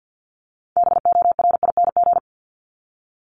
Morse code, heard as a keyed tone, about 50 wpm, 700 Hz. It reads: BORING
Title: Top 1501-1750 Words 50wpm